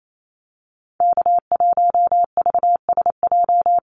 Morse code, heard as a keyed tone, about 28 wpm, 700 Hz. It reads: K1VSJ
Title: Call Signs - 28wpm